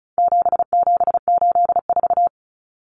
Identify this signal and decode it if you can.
Morse code, heard as a keyed tone, about 34 wpm, 700 Hz. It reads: 7784
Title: Common Cut Numbers - 4-Digit 35wpm